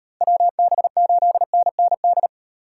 Morse code, heard as a keyed tone, about 38 wpm, 700 Hz. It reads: WB8NND